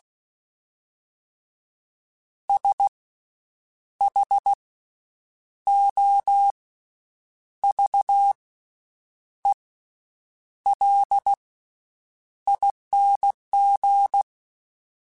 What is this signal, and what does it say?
Morse code, heard as a keyed tone, about 16 wpm, spaced as for 7 wpm, 800 Hz. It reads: SHOVELING
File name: Special INGs / Special INGs-mp3s / VerySlowToFaster Ing @15